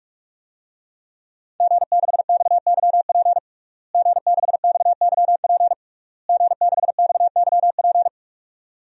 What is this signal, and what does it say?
Morse code, heard as a keyed tone, about 44 wpm, 700 Hz. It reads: G6XYP G6XYP G6XYP
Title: DXCC-100 Call Signs - 5 Characters - Repeated 3x - 45wpm